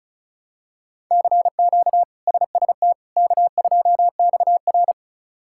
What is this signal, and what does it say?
Morse code, heard as a keyed tone, about 34 wpm, 700 Hz. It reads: CQ SST K2XR
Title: K1USN SST QSO Practice - 35wpm